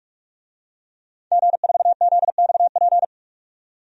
Morse code, heard as a keyed tone, about 46 wpm, 700 Hz. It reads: G4ZXP
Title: DXCC-100 Call Signs - 5 Characters - 45wpm